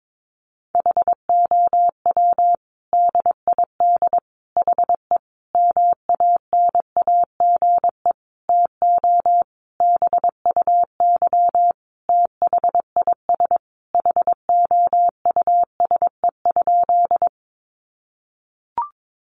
Morse code, heard as a keyed tone, about 22 wpm, 700 Hz. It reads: HOW DID HE MANAGE TO BUY THIS HOUSE?
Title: Sentences from Top 1400 Words 22wpm